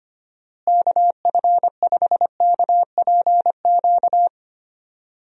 Morse code, heard as a keyed tone, about 25 wpm, 700 Hz. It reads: KF5KPQ
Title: USA Call Signs - 6 Characters - Repeated 2x - 25wpm